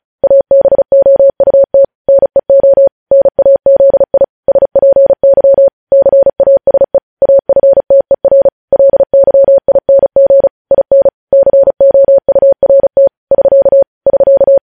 Morse code, heard as a keyed tone, about 36 wpm, 550 Hz. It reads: ABOUT NEO NAZI SPY CASE AFTER LYING IN COURT <SK> <SK>
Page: news headlines in morse code at 35 wpm